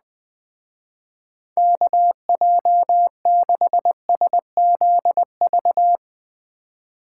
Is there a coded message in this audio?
Yes: Morse code from 1.6 to 6.0 s, heard as a keyed tone, about 21 wpm, 700 Hz. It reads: KJ6SZV